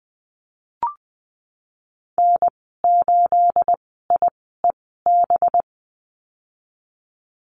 Morse code, heard as a keyed tone, about 20 wpm, spaced as for 14 wpm, 700 Hz. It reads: N8IEB
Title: 245 Lesson 39g - Letter z - Call Signs 20wpm